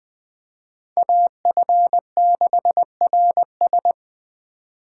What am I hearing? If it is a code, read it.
Morse code, heard as a keyed tone, about 21 wpm, 700 Hz. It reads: AF6RS